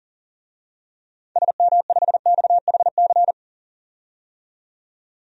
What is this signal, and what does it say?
Morse code, heard as a keyed tone, about 37 wpm, 700 Hz. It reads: SM5XHC
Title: DXCC-100 Call Signs - 6 Characters - 40wpm